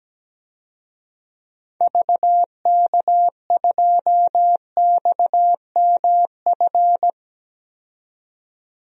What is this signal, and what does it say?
Morse code, heard as a keyed tone, about 17 wpm, 700 Hz. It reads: VK2XMF
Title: DXCC-100 Call Signs - 6 Characters - 17wpm